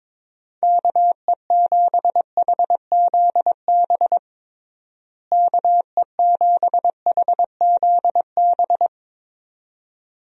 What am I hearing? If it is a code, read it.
Morse code, heard as a keyed tone, about 22 wpm, 700 Hz. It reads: KE7HZB KE7HZB
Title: USA Call Signs - 6 Characters - Repeated 2x - 22wpm